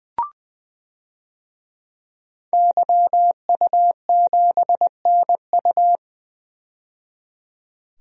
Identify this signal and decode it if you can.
Morse code, heard as a keyed tone, about 20 wpm, 700 Hz. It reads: YU7NU
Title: DXCC-100 Call Signs - 5 Characters - 20wpm